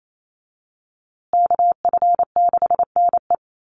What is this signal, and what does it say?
Morse code, heard as a keyed tone, about 28 wpm, 700 Hz. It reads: KF6NE
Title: USA Call Signs Repeated 3x 28wpm